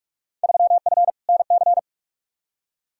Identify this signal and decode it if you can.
Morse code, heard as a keyed tone, about 44 wpm, 700 Hz. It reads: 3F NC
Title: Field Day Exchanges - 45wpm